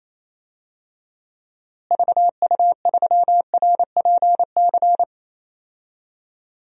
Morse code, heard as a keyed tone, about 27 wpm, 700 Hz. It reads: VU3RPC
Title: DXCC-100 Call Signs - 6 Characters - Repeated 2x - 28wpm